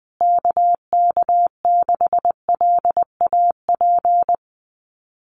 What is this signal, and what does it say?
Morse code, heard as a keyed tone, about 20 wpm, 700 Hz. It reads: KK6LAP
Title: USA Call Signs Repeated 2x 20wpm